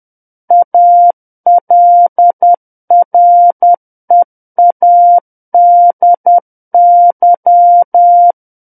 Morse code, heard as a keyed tone, about 10 wpm, 700 Hz. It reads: ALREADY
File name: Book One / 01-Chapter 1 - 10 WPM